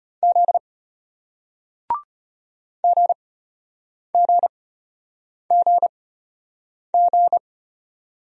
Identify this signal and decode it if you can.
Morse code, heard as a keyed tone, about 34 wpm, spaced as for 8 wpm, 700 Hz. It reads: Z GGGG